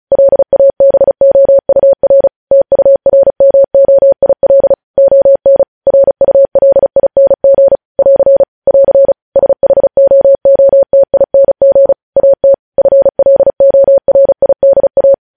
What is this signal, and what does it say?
Morse code, heard as a keyed tone, about 36 wpm, 550 Hz. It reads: LABOUR TURMOIL ON RULING <AR> <AR> SHOOTING AT FLORIDA
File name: podcast / News headlines-1x-35x35